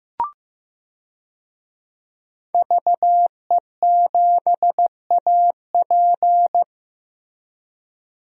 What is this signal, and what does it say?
Morse code, heard as a keyed tone, about 16 wpm, 700 Hz. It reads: VE7AP
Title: DXCC-100 Call Signs - 5 Characters - Repeated 3x - 15wpm